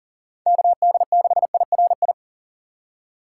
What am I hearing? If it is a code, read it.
Morse code, heard as a keyed tone, about 40 wpm, 700 Hz. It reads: KD6IRI
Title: K1USN SST QSO Practice - 40wpm